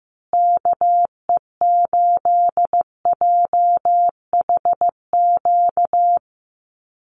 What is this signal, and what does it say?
Morse code, heard as a keyed tone, about 15 wpm, 700 Hz. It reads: KE8JHQ
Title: USA Call Signs Repeated 2x 15wpm